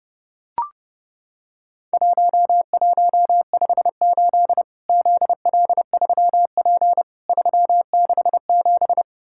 Morse code, heard as a keyed tone, about 30 wpm, 700 Hz. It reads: 1158 ZL3P 367
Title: Worked All Europe DX Contest - QTC Exchange - 30wpm